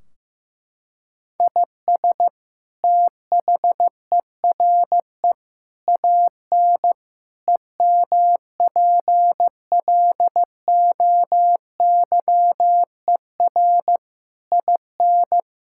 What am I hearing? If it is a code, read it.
Morse code, heard as a keyed tone, about 15 wpm, 700 Hz. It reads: IS THERE AN EMPLOYER IN
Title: Sentences from Top 1600 Words 15wpm